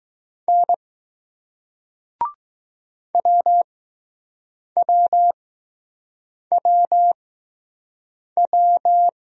Morse code, heard as a keyed tone, about 20 wpm, spaced as for 6 wpm, 700 Hz. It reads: N WWWW